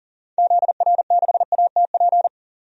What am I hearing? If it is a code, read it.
Morse code, heard as a keyed tone, about 40 wpm, 700 Hz. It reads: ZR6ATP
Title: DXCC-100 Call Signs - 6 Characters - 40wpm